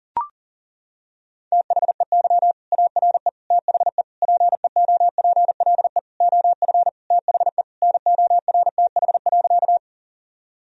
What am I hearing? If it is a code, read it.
Morse code, heard as a keyed tone, about 38 wpm, 700 Hz. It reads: THEY ARE THE PEOPLE OF THE NORTH.
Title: Sentences from Top 1000 Words - Review Part 1 - 40wpm